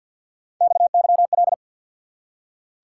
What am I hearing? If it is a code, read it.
Morse code, heard as a keyed tone, about 47 wpm, 700 Hz. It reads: XYL
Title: Top 90 QSO Elements 50wpm